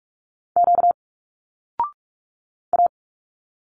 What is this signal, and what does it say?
Morse code, heard as a keyed tone, about 46 wpm, 700 Hz. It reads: Q A
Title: Single Letters 45wpm